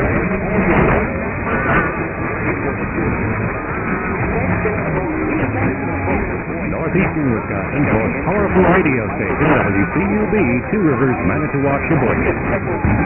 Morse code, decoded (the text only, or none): none